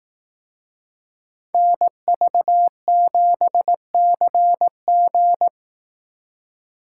Morse code, heard as a keyed tone, about 18 wpm, 700 Hz. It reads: NV7CG
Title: Speed Racing - USA 2x2 Call Signs 15wpm